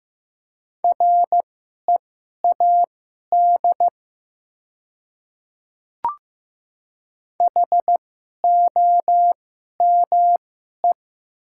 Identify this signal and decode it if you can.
Morse code, heard as a keyed tone, about 15 wpm, spaced as for 11 wpm, 700 Hz. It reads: READ HOME
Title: 2x Character Spacing - Top 200 Words - 15wpm